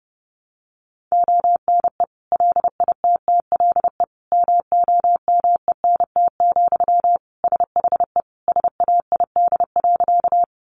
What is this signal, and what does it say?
Morse code, heard as a keyed tone, about 30 wpm, 700 Hz. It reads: ONE LITTLE MOMENT, SHE SAID.
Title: Sentences from Top 1000 Words 30wpm